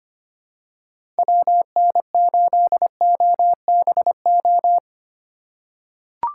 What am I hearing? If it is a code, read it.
Morse code, heard as a keyed tone, about 25 wpm, 700 Hz. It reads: WN8OBO
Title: USA Call Signs - 3x Character Spacing - 25wpm